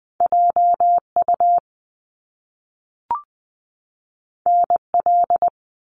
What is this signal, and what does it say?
Morse code, heard as a keyed tone, about 20 wpm, 700 Hz. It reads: J U N L
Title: Top Two Letter Combinations in Words 20wpm